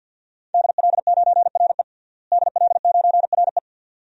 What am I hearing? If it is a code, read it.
Morse code, heard as a keyed tone, about 53 wpm, 700 Hz. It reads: DL9RE DL9RE
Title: DXCC-100 Call Signs - 5 Characters - Repeated 2x - 50wpm